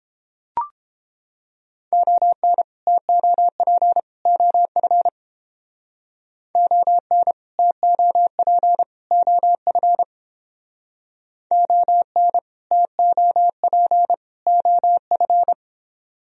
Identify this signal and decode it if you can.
Morse code, heard as a keyed tone, about 30 wpm, 700 Hz. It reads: ON TOP OF ON TOP OF ON TOP OF